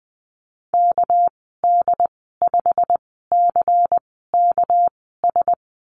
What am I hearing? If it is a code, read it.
Morse code, heard as a keyed tone, about 20 wpm, spaced as for 14 wpm, 700 Hz. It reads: KD5CKS
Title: 219 Lesson 35g - Number 0 - Call Signs 20wpm